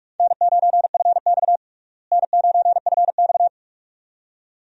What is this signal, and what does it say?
Morse code, heard as a keyed tone, about 41 wpm, 700 Hz. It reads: N9FX N9FX
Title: USA Call Signs - 4 Characters - Repeated 2x - 45wpm